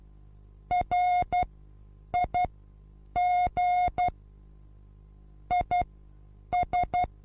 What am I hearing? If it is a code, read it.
Morse code, heard as a keyed tone, about 12 wpm, spaced as for 8 wpm, 720 Hz. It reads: RIG IS
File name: Onlinelehrgang / morsen / DJ4UF Morsen23